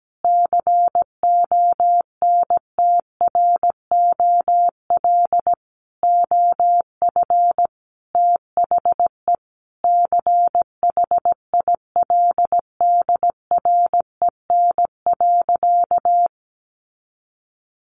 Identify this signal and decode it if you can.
Morse code, heard as a keyed tone, about 17 wpm, 700 Hz. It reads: CONTROL OF THE CHILDREN.
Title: Sentences from Top 1200 Words 17wpm